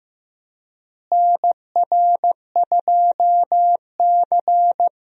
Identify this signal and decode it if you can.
Morse code, heard as a keyed tone, about 15 wpm, 700 Hz. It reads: NR2C
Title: USA Call Signs - 4 Characters - Repeated 3x - 15wpm